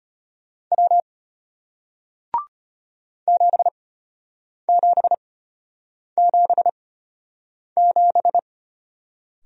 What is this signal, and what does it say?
Morse code, heard as a keyed tone, about 33 wpm, spaced as for 8 wpm, 700 Hz. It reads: W 7777